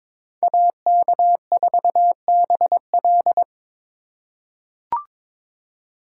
Morse code, heard as a keyed tone, about 22 wpm, 700 Hz. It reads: AK4BL
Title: USA Call Signs - 5 Characters - 2x Character Spacing - 22wpm